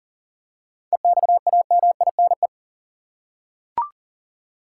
Morse code, heard as a keyed tone, about 40 wpm, 700 Hz. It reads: EXAMINE
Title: Seven Letter Words - 40wpm